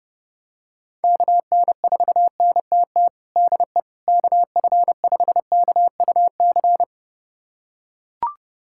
Morse code, heard as a keyed tone, about 32 wpm, 700 Hz. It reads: KN4NTT DE KF5KUC
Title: USA Callsign DE Callsign 30wpm